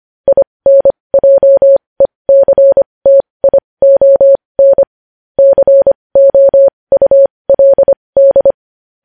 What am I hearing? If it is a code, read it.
Morse code, heard as a keyed tone, about 25 wpm, spaced as for 20 wpm, 550 Hz. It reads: INJECTION COULD
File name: podcast / News headlines-1x-25x20